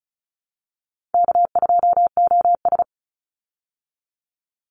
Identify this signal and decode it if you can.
Morse code, heard as a keyed tone, about 34 wpm, 700 Hz. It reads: K2OS
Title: Easy Call Signs Repeated 2x - 35wpm